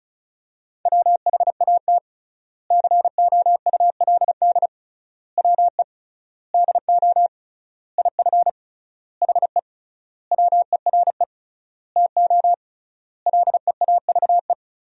Morse code, heard as a keyed tone, about 36 wpm, 700 Hz. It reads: WHAT COULD WE DO IF HE WERE TO LEAVE
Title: 3x Word-Spacing - Sentences from Top 400 Words - 35wpm